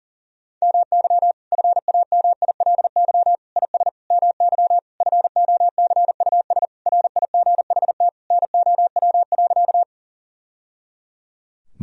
Morse code, heard as a keyed tone, about 40 wpm, 700 Hz. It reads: MY FAMILY IS MY FOCUS RIGHT NOW.
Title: Sentences from Top 1000 Words - Review Part 4 - 40wpm